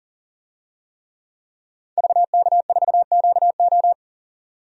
Morse code, heard as a keyed tone, about 40 wpm, 700 Hz. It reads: VK4QO
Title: DXCC-100 Call Signs - 5 Characters - 40wpm